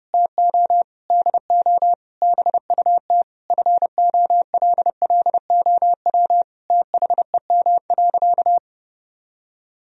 Morse code, heard as a keyed tone, about 30 wpm, 700 Hz. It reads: TO DO BUT FOLLOW THEM.
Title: Sentences from Top 1000 Words - Review Part 1 - 30wpm